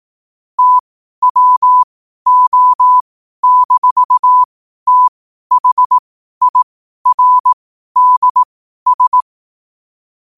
Morse code, heard as a keyed tone, about 18 wpm, spaced as for 12 wpm, 1000 Hz. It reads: TWO-THIRDS
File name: qotdfiles / 12wpm